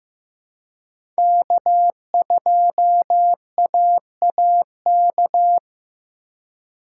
Morse code, heard as a keyed tone, about 15 wpm, 700 Hz. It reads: K2AAK
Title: USA Call Signs - 5 Characters - 4x Character Spacing - 15wpm